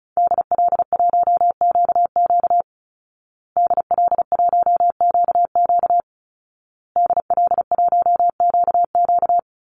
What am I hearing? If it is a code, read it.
Morse code, heard as a keyed tone, about 34 wpm, 700 Hz. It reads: DL1QQ DL1QQ DL1QQ
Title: Call Signs Repeated 2x - 35wpm